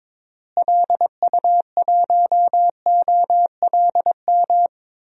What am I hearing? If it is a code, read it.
Morse code, heard as a keyed tone, about 22 wpm, 700 Hz. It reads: LU1OLM
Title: DXCC-100 Call Signs - 6 Characters - Repeated 3x - 22wpm